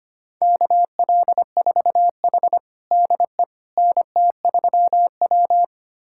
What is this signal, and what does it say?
Morse code, heard as a keyed tone, about 25 wpm, 700 Hz. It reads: KL4H DE NT3W
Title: USA Callsign DE Callsign - 4 Characters - 25wpm